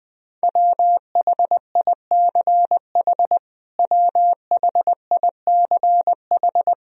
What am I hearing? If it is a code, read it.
Morse code, heard as a keyed tone, about 21 wpm, 700 Hz. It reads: WHICH WHICH
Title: Top 100 Words Repeated 20wpm